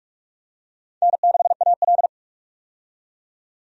Morse code, heard as a keyed tone, about 46 wpm, 700 Hz. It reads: N6AL
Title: USA Call Signs - 4 Characters - 45wpm